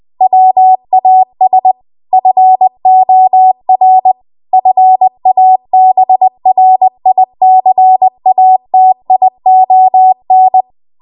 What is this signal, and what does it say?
Morse code, heard as a keyed tone, about 20 wpm, 750 Hz. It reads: WAS FOR FABRICATION
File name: Archive / 20 WPM / 20WPM